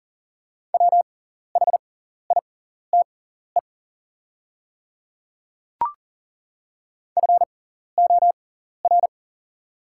Morse code, heard as a keyed tone, about 40 wpm, spaced as for 14 wpm, 690 Hz. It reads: WHITE FOR
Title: 6x Character Spacing - Top 300 Words - 40wpm